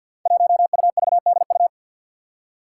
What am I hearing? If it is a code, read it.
Morse code, heard as a keyed tone, about 49 wpm, 700 Hz. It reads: 1UVDU
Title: Five Challenging Letter-Number Combinations 50wpm